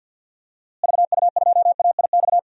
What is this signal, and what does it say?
Morse code, heard as a keyed tone, about 47 wpm, 700 Hz. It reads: VU2AIX
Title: DXCC-100 Call Signs - 6 Characters - Repeated 2x - 50wpm